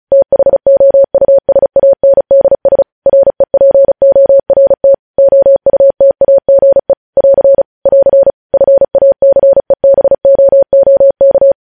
Morse code, heard as a keyed tone, about 36 wpm, 550 Hz. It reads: THOUSANDS REPORT OUTAGE <AR> <AR> FACEBOOK
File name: podcast / News headlines-1x-35x35